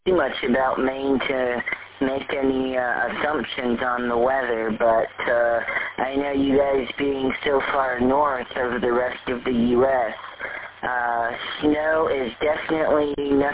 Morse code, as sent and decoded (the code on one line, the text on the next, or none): none